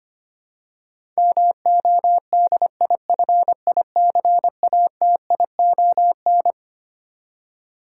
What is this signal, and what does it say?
Morse code, heard as a keyed tone, about 25 wpm, 700 Hz. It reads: MODIFICATION
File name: Twelve Letter Words - 25wpm